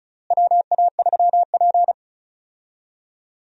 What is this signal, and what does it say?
Morse code, heard as a keyed tone, about 34 wpm, 700 Hz. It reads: WA3P
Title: USA Call Signs - 4 Characters - Repeated 3x - 35wpm